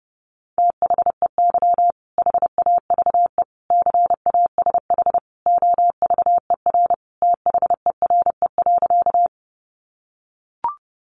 Morse code, heard as a keyed tone, about 30 wpm, 700 Hz. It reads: THEY HAVE CASH OVER THERE.
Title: Sentences from Top 700 Words 30wpm